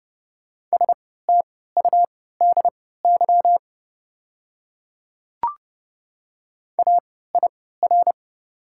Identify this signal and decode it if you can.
Morse code, heard as a keyed tone, about 30 wpm, spaced as for 17 wpm, 700 Hz. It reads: STUDY AIR
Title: 3x Character Spacing - Top 300 Words - 30wpm